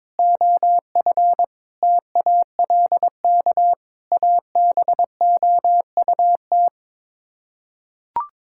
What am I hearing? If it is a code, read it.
Morse code, heard as a keyed tone, about 22 wpm, 700 Hz. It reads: OF TALK ABOUT
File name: practice / 22wpm / Sets of 8 Words - 22wpm